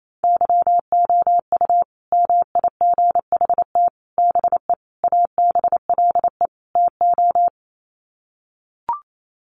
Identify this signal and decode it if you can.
Morse code, heard as a keyed tone, about 28 wpm, 700 Hz. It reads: YOU MIGHT BE ABLE TO
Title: Sets of 5 Words Encore2 28wpm